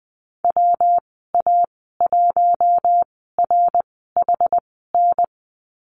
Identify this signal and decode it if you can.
Morse code, heard as a keyed tone, about 19 wpm, spaced as for 14 wpm, 700 Hz. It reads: WA1RHN
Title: 122 Lesson 20g - Number 3 - Call Signs 20wpm